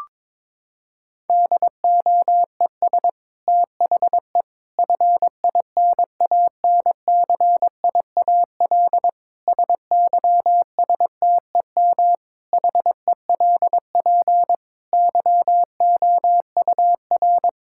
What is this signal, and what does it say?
Morse code, heard as a keyed tone, about 22 wpm, 700 Hz. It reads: DOES THE FINANCIAL SYSTEM HELP YOUR
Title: Sentences from Top 1000 Words - Review Part 5 - 22wpm